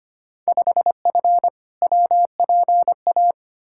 Morse code, heard as a keyed tone, about 25 wpm, 700 Hz. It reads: 5F WPA
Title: Field Day Exchanges Encore - 25wpm